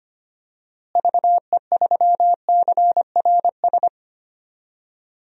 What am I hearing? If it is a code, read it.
Morse code, heard as a keyed tone, about 25 wpm, 700 Hz. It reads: VE3CRS
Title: DXCC-100 Call Signs - 6 Characters - Repeated 2x - 25wpm